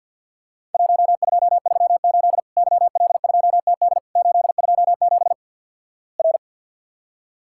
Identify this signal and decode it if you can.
Morse code, heard as a keyed tone, about 47 wpm, 700 Hz. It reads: 1238 YL2TD 827 R
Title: Worked All Europe DX Contest - QTC Exchange - 50wpm